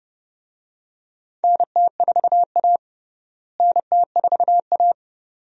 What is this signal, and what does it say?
Morse code, heard as a keyed tone, about 30 wpm, 700 Hz. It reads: NT4A NT4A
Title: USA Call Signs - 4 Characters - Repeated 2x - 30wpm